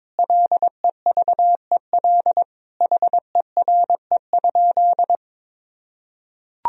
Morse code, heard as a keyed tone, about 22 wpm, 700 Hz. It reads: LEVEL HERE?
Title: Sentences from Top 1000 Words - Review Part 4 - 22wpm